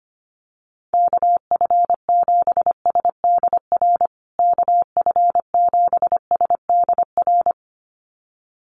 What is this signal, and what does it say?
Morse code, heard as a keyed tone, about 25 wpm, 700 Hz. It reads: KF7SDR KF7SDR
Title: USA Call Signs Repeated 2x 25wpm